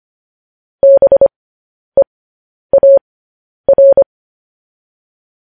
Morse code, heard as a keyed tone, about 25 wpm, spaced as for 10 wpm, 550 Hz. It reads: BEAR
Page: news headlines in morse code at 10 wpm